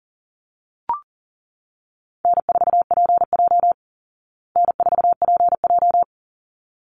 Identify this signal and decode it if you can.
Morse code, heard as a keyed tone, about 40 wpm, 700 Hz. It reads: N4PJ N4PJ
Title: Easy Call Signs Repeated 2x - 40wpm